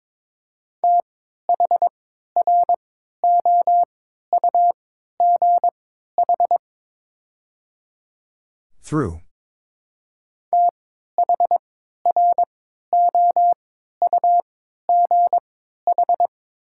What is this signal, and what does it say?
Morse code, heard as a keyed tone, about 23 wpm, spaced as for 13 wpm, 700 Hz. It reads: THROUGH THROUGH
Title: 3x Character Spacing - Top 200 Words - 22wpm